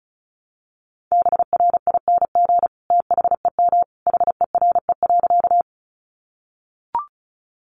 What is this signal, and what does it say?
Morse code, heard as a keyed tone, about 34 wpm, 700 Hz. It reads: BRING THEM HERE.
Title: Sentences from Top 500 Words 35wpm